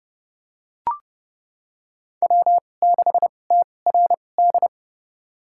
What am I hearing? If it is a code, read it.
Morse code, heard as a keyed tone, about 30 wpm, spaced as for 22 wpm, 700 Hz. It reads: W6TRD
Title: USA Call Signs - 5 Characters - 2x Character Spacing - 30wpm